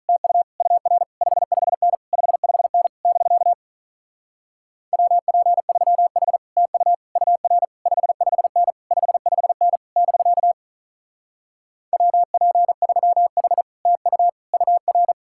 Morse code, heard as a keyed tone, about 46 wpm, 700 Hz. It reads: TU UR 55N 55N <BK> WP3H TU UR 55N 55N <BK> WP3H TU UR